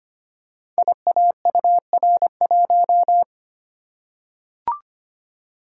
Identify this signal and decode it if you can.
Morse code, heard as a keyed tone, about 25 wpm, 700 Hz. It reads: IAUR1
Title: Code Groups - 5 Letter:Numbers - 25wpm